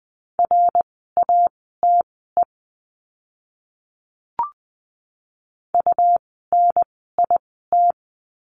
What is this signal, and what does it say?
Morse code, heard as a keyed tone, about 20 wpm, spaced as for 14 wpm, 700 Hz. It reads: RATE UNIT
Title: 101 Lesson 17f - Letter u - Words 20wpm